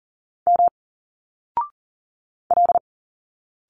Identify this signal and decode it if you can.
Morse code, heard as a keyed tone, about 39 wpm, 700 Hz. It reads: M L
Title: Single Letters 40wpm